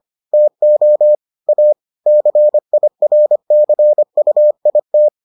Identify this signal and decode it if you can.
Morse code, heard as a keyed tone, about 25 wpm, 600 Hz. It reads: TO A CIRCUIT